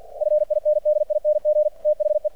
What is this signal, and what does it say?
Morse code, heard as a keyed tone, about 50 wpm, 600 Hz. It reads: FINDING THE
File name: archived / CRSnet